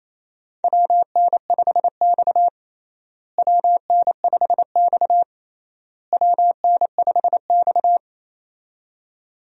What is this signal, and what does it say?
Morse code, heard as a keyed tone, about 28 wpm, 700 Hz. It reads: WN5X WN5X WN5X
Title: USA Call Signs - 4 Characters - Repeated 3x - 28wpm